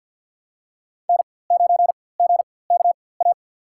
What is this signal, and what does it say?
Morse code, heard as a keyed tone, about 52 wpm, spaced as for 23 wpm, 700 Hz. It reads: N9GKA
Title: USA Call Signs - 4x Character Spacing - 50wpm